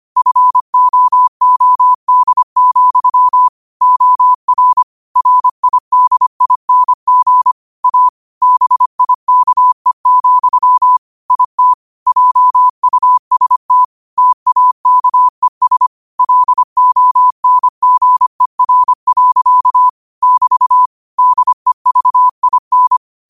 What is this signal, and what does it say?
Morse code, heard as a keyed tone, about 25 wpm, 1000 Hz. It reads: FOOD, OR RIDING A BIKE, IT JUST TAKES LONGER. <BT> DEVIN
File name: qotdfiles / 25wpm